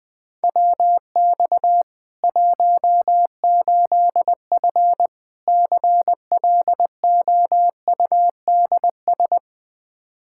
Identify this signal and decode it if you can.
Morse code, heard as a keyed tone, about 20 wpm, 700 Hz. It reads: WX 18F CLOUDS
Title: Practice QSO - Exchange 2 - Info - 20wpm